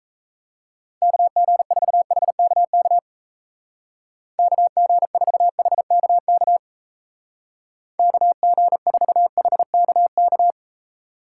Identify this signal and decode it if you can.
Morse code, heard as a keyed tone, about 35 wpm, 700 Hz. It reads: KG4HKK KG4HKK KG4HKK